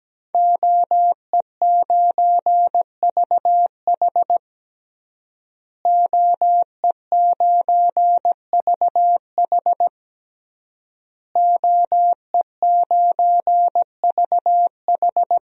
Morse code, heard as a keyed tone, about 17 wpm, 700 Hz. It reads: OE9VH OE9VH OE9VH
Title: DXCC-100 Call Signs - 5 Characters - Repeated 3x - 17wpm